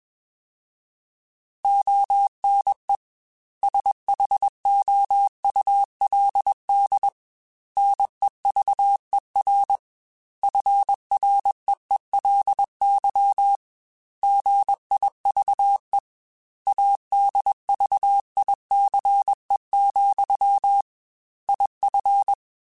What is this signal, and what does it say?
Morse code, heard as a keyed tone, about 21 wpm, 800 Hz. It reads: ONE SHOULD NEVER FREELY GIVE ADVICE, IF
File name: Quotes / Quotes-mp3s / Advice @20